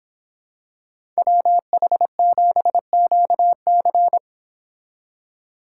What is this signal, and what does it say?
Morse code, heard as a keyed tone, about 26 wpm, 700 Hz. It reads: WH7QC